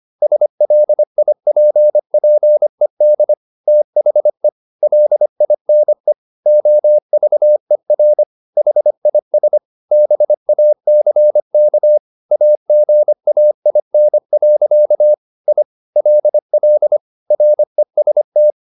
Morse code, heard as a keyed tone, about 25 wpm, 600 Hz. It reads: SLIPPED THE LINE OVER HIS BACK AGAIN. I LL REST